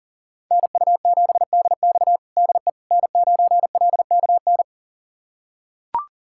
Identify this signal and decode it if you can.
Morse code, heard as a keyed tone, about 40 wpm, 700 Hz. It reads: NU7DX DE N9LKN
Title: USA Callsign DE Callsign - 5 Characters - 40wpm